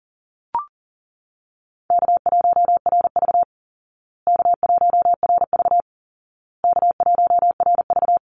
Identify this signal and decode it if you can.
Morse code, heard as a keyed tone, about 40 wpm, 700 Hz. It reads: K1RV K1RV K1RV
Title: Easy Call Signs Repeated 3x - 40wpm